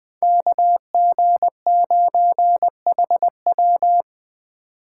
Morse code, heard as a keyed tone, about 23 wpm, 700 Hz. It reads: KG9HW
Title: Speed Racing - USA 2x2 Call Signs 20wpm